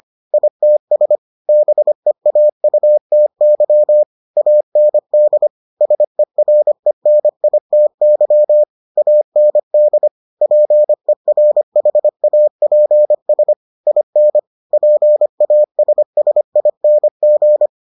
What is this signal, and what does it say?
Morse code, heard as a keyed tone, about 25 wpm, 600 Hz. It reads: ITS BEAUTY AND SERENITY AND PERHAPS IN PASSING